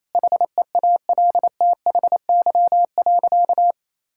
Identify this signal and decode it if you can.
Morse code, heard as a keyed tone, about 28 wpm, 700 Hz. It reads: HEALTHY.
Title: Sentences from Top 1000 Words - Review Part 5 - 28wpm